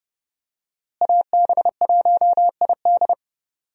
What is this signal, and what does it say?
Morse code, heard as a keyed tone, about 30 wpm, 700 Hz. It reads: AB1ID
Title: USA Call Signs - 5 Characters - Repeated 3x - 30wpm